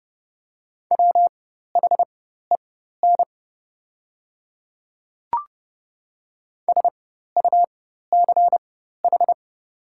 Morse code, heard as a keyed tone, about 30 wpm, spaced as for 14 wpm, 700 Hz. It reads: WHEN SUCH
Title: 4x Character Spacing - Top 300 Words - 30wpm